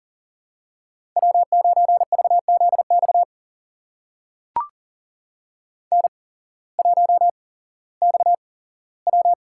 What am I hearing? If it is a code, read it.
Morse code, heard as a keyed tone, about 40 wpm, spaced as for 19 wpm, 700 Hz. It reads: W9VZX N1XW
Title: USA Call Signs - 8x Character Spacing - 40wpm